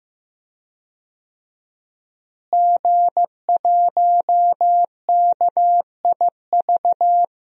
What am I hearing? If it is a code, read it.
Morse code, heard as a keyed tone, about 15 wpm, 700 Hz. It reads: G1KIV
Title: DXCC-100 Call Signs - 5 Characters - Repeated 2x - 15wpm